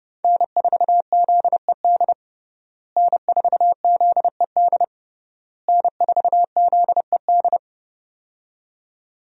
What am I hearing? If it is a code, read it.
Morse code, heard as a keyed tone, about 32 wpm, 700 Hz. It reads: N4ZED N4ZED N4ZED
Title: USA Call Signs - 5 Characters - Repeated 3x - 30wpm